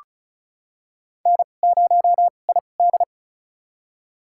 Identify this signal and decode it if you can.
Morse code, heard as a keyed tone, about 36 wpm, spaced as for 25 wpm, 700 Hz. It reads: N0ID
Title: USA Call Signs - 2x Character Spacing - 35wpm